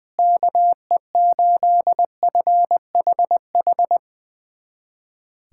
Morse code, heard as a keyed tone, about 21 wpm, 700 Hz. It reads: KE8FHH
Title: USA Callsign DE Callsign 20wpm